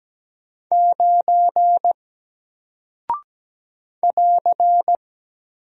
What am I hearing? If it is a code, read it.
Morse code, heard as a keyed tone, about 17 wpm, 700 Hz. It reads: 9 <AR>
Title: Single Character (Letter:Number:Punctuation:Prosign) 17wpm